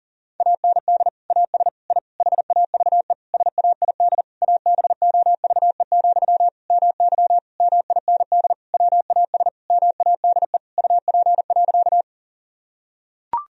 Morse code, heard as a keyed tone, about 40 wpm, 700 Hz. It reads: AND AS I HAVE SAID ABOVE, MY MIND WAS MADE UP.
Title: Sentences from Top 1000 Words - Review Part 2 - 40wpm